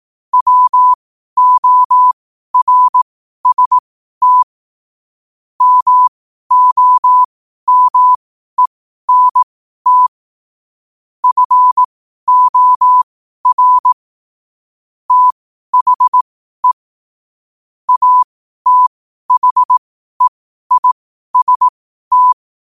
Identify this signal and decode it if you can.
Morse code, heard as a keyed tone, about 18 wpm, spaced as for 12 wpm, 1000 Hz. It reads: WORST MOMENT FOR THE ATHEIST
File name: qotdfiles / 12wpm